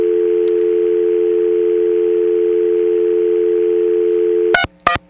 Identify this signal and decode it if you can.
Dial tone (North America), 350 + 440 Hz.
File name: TurtleCreekEWSD